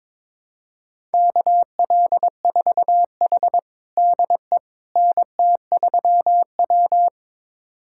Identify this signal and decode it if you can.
Morse code, heard as a keyed tone, about 22 wpm, 700 Hz. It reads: KL4H DE NT3W
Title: USA Callsign DE Callsign - 4 Characters - 22wpm